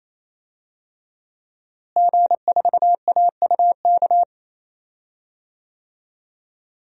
Morse code, heard as a keyed tone, about 28 wpm, 700 Hz. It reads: G4AUK
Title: DXCC-100 Call Signs - 5 Characters - Repeated 3x - 28wpm